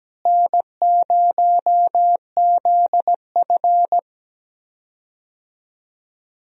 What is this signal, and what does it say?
Morse code, heard as a keyed tone, about 17 wpm, 700 Hz. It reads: N0ZF
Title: USA Call Signs - 4 Characters - Repeated 3x - 17wpm